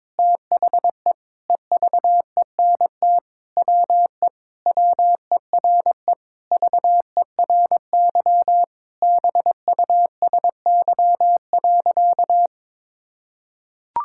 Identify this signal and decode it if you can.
Morse code, heard as a keyed tone, about 22 wpm, 700 Hz. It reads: THE EVENT WE WERE VERY BUSY.
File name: Sentences with 7-Word Phrases - 22wpm